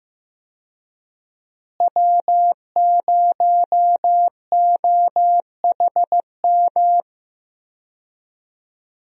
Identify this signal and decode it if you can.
Morse code, heard as a keyed tone, about 15 wpm, 700 Hz. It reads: W0OHM
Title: USA Call Signs - 5 Characters - Repeated 2x - 15wpm